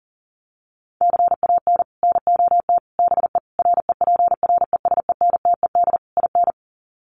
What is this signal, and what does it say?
Morse code, heard as a keyed tone, about 40 wpm, 700 Hz. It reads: CAN NOT BE REPRESENTED IN
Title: Sets of 5 Words 40wpm